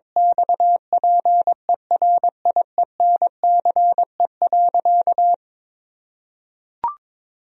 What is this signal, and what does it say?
Morse code, heard as a keyed tone, about 22 wpm, 700 Hz. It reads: XPERIENCE.
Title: Sentences from Top 1000 Words - Review Part 3 - 22wpm